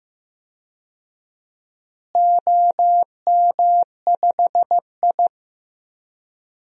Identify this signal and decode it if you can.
Morse code, heard as a keyed tone, about 15 wpm, 700 Hz. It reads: OM5I
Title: DXCC-100 Call Signs - 4 Characters - Repeated 3x - 15wpm